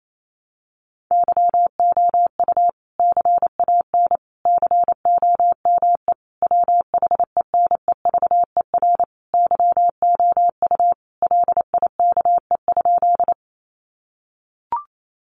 Morse code, heard as a keyed tone, about 28 wpm, 700 Hz. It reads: YOU CAN COME WHENEVER YOU LIKE?
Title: Sentences from Top 1200 Words 28wpm